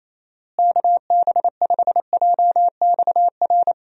Morse code, heard as a keyed tone, about 28 wpm, 700 Hz. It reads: KB5JXR
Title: USA Call Signs - 6 Characters - Repeated 3x - 28wpm